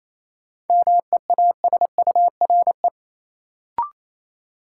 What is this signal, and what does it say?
Morse code, heard as a keyed tone, about 28 wpm, 700 Hz. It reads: MEASURE
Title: Sentences from Top 1000 Words - Review Part 1 - New Word Drill - 28wpm